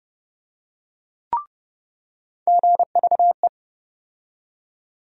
Morse code, heard as a keyed tone, about 30 wpm, 700 Hz. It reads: GVE
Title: USA Call Sign Postfixes - 3 Letters 30wpm